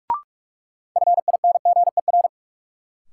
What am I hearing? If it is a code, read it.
Morse code, heard as a keyed tone, about 41 wpm, 700 Hz. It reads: FINGER
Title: Sentences from Top 1600 Words - New Word Drill - 45wpm